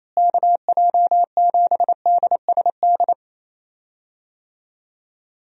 Morse code, heard as a keyed tone, about 28 wpm, 700 Hz. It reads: KJ7DSD
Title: USA Call Signs - 6 Characters - Repeated 3x - 28wpm